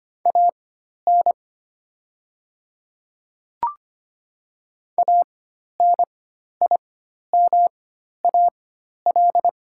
Morse code, heard as a keyed tone, about 25 wpm, spaced as for 12 wpm, 700 Hz. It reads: AN ANIMAL
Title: 4x Character Spacing - Top 200 Words - 25wpm